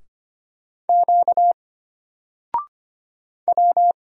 Morse code, heard as a keyed tone, about 25 wpm, 700 Hz. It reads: Q W
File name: Single-Letter-Number-25wpm